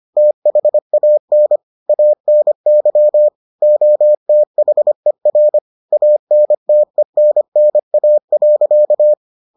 Morse code, heard as a keyed tone, about 25 wpm, 600 Hz. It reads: THAN ANY OTHER ANTENNA.